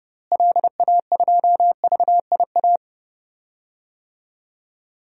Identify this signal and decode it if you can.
Morse code, heard as a keyed tone, about 31 wpm, 700 Hz. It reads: LA2VIA
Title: DXCC-100 Call Signs - 6 Characters - Repeated 3x - 30wpm